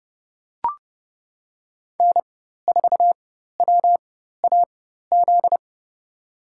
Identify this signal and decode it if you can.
Morse code, heard as a keyed tone, about 31 wpm, spaced as for 14 wpm, 700 Hz. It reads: N4WAZ
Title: USA Call Signs - 5 Characters - 4x Character Spacing - 30wpm